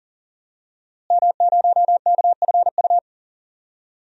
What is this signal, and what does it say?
Morse code, heard as a keyed tone, about 40 wpm, 700 Hz. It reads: M0KFU
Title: DXCC-100 Call Signs - 5 Characters - Repeated 2x - 40wpm